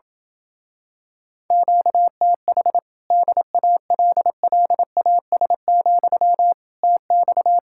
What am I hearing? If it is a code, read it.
Morse code, heard as a keyed tone, about 27 wpm, 700 Hz. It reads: QTH DALLAS, TX